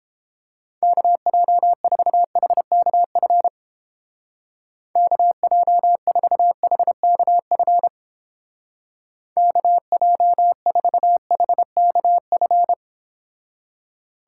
Morse code, heard as a keyed tone, about 29 wpm, 700 Hz. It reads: KJ4HKF KJ4HKF KJ4HKF